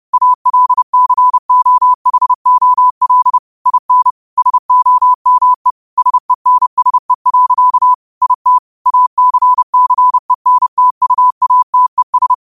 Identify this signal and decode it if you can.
Morse code, heard as a keyed tone, about 30 wpm, 1000 Hz. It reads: ALCOHOL IN SOME SENSE. IT ACCENTUATES
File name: qotdfiles / 30wpm